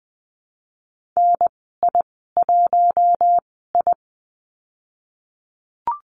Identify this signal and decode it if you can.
Morse code, heard as a keyed tone, about 20 wpm, spaced as for 15 wpm, 700 Hz. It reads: NI1I
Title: 055 Lesson 10g - Letter r - Call Signs 20wpm